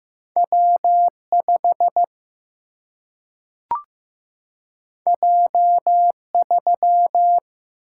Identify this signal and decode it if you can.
Morse code, heard as a keyed tone, about 15 wpm, 700 Hz. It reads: W5 J3
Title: Single-Letter Single-Number 15wpm